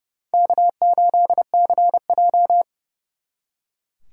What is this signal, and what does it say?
Morse code, heard as a keyed tone, about 36 wpm, 700 Hz. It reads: K8CJ
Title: Speed Racing - USA Call Signs 30wpm